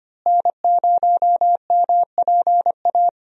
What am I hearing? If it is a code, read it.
Morse code, heard as a keyed tone, about 25 wpm, 700 Hz. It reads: N0MPA
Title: USA Call Signs - 5 Characters - Repeated 2x - 25wpm